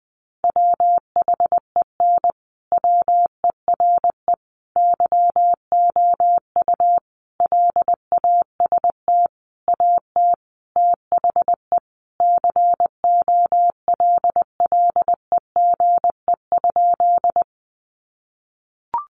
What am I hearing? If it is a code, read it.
Morse code, heard as a keyed tone, about 19 wpm, 700 Hz. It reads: WHEN WERE YOU LAST AT THE COLLEGE?
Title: Sentences from Top 1300 Words 20wpm